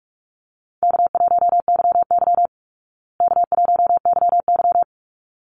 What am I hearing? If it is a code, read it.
Morse code, heard as a keyed tone, about 44 wpm, 700 Hz. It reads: K1YY K1YY
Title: Easy Call Signs Repeated 2x - 45wpm